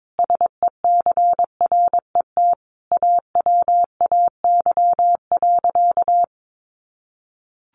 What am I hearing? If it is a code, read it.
Morse code, heard as a keyed tone, about 22 wpm, 700 Hz. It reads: SECRET AWAY.
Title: Sentences from Top 1200 Words 22wpm